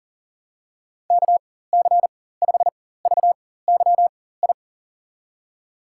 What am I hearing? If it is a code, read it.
Morse code, heard as a keyed tone, about 40 wpm, spaced as for 19 wpm, 700 Hz. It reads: KC5VYI
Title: USA Call Signs - 6 Characters - 4x Character Spacing - 40wpm